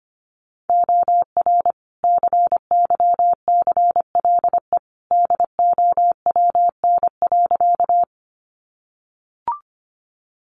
Morse code, heard as a keyed tone, about 25 wpm, 700 Hz. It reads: OR CYCLE DOWN.
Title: Sentences from Top 900 Words 25wpm